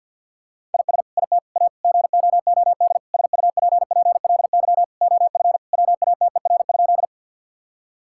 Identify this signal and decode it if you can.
Morse code, heard as a keyed tone, about 50 wpm, 700 Hz. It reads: IS IT A GOOD SUPPLY OF WATER?
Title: Sentences from Top 1000 Words - Review Part 5 - 50wpm